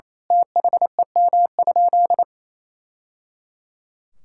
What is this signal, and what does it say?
Morse code, heard as a keyed tone, about 27 wpm, 700 Hz. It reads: THEM?
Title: Sentences from Top 1000 Words - Review Part 6 - 28wpm